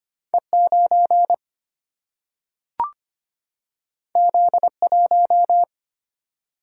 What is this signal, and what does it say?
Morse code, heard as a keyed tone, about 25 wpm, 700 Hz. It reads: E9 Z1
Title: Single-Letter Single-Number 25wpm